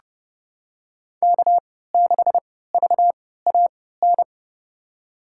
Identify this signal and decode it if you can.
Morse code, heard as a keyed tone, about 30 wpm, spaced as for 17 wpm, 700 Hz. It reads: K6VAN
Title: USA Call Signs - 3x Character Spacing - 30wpm